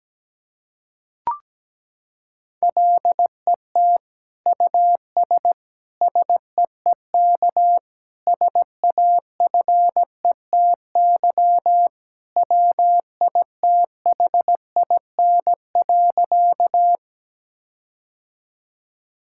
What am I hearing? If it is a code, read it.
Morse code, heard as a keyed tone, about 17 wpm, 700 Hz. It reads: LET US SEEK SAFETY WITHIN.
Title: Sentences from Top 1600 Words 17wpm